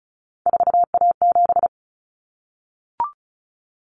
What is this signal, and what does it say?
Morse code, heard as a keyed tone, about 34 wpm, 700 Hz. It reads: 4A7
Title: Common Cut Numbers - 3-Digit 35wpm